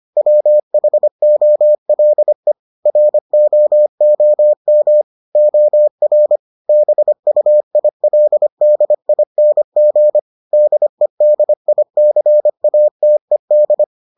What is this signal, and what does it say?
Morse code, heard as a keyed tone, about 25 wpm, 600 Hz. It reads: WHOLE ROOM OR BUILDING DEDICATED